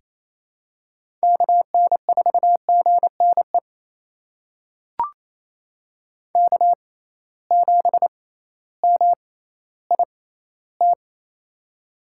Morse code, heard as a keyed tone, about 27 wpm, spaced as for 15 wpm, 700 Hz. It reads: KN4GNE K7MIT
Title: USA Call Signs - 6x Character Spacing - 28wpm